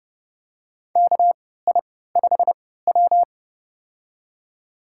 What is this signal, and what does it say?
Morse code, heard as a keyed tone, about 31 wpm, spaced as for 17 wpm, 700 Hz. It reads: KI5W
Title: USA Call Signs - 3x Character Spacing - 30wpm